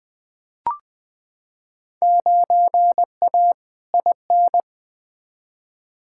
Morse code, heard as a keyed tone, about 20 wpm, 700 Hz. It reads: 9A IN
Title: Field Day Exchanges Encore - 20wpm